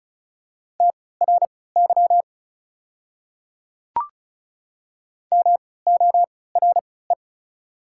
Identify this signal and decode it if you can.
Morse code, heard as a keyed tone, about 36 wpm, spaced as for 20 wpm, 690 Hz. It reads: TRY MORE
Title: 3x Character Spacing - Top 300 Words - 35wpm